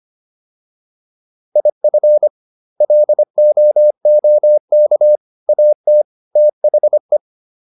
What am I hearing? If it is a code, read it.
Morse code, heard as a keyed tone, about 25 wpm, 600 Hz. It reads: IF LOOK AT THE